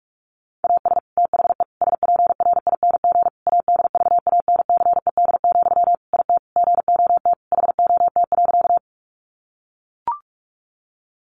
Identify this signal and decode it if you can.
Morse code, heard as a keyed tone, about 44 wpm, 700 Hz. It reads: AS THE SPRING ADVANCED, IT GOT HOT.
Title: Sentences from Top 1200 Words 45wpm